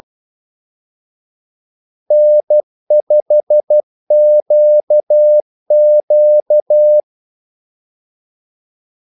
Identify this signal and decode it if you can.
Morse code, heard as a keyed tone, about 12 wpm, 600 Hz. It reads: N5QQ
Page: 2 ekstra ordmellemrum mellem kaldesignalerne